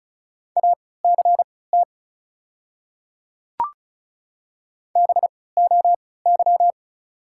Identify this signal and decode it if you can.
Morse code, heard as a keyed tone, about 36 wpm, spaced as for 20 wpm, 700 Hz. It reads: ACT BOY
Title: 3x Character Spacing - Top 200 Words - 35wpm